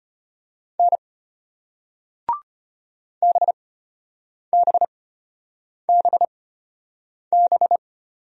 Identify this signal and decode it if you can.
Morse code, heard as a keyed tone, about 33 wpm, spaced as for 8 wpm, 700 Hz. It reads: N BBBB